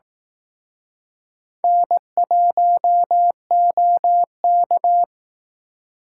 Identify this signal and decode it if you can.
Morse code, heard as a keyed tone, about 18 wpm, 700 Hz. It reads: N1OK
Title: Speed Racing - USA 1x2 Call Signs 15wpm